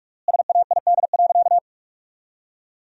Morse code, heard as a keyed tone, about 47 wpm, 700 Hz. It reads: SAID.
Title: 4x Word-Spacing - Sentences from Top 200 Words - 45wpm